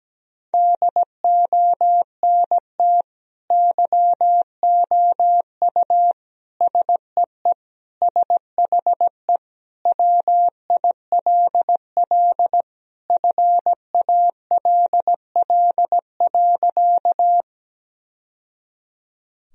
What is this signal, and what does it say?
Morse code, heard as a keyed tone, about 17 wpm, 700 Hz. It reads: DONT YOU SEE SHE WILL FALL.
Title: Sentences from Top 1000 Words - Review Part 2 - 17wpm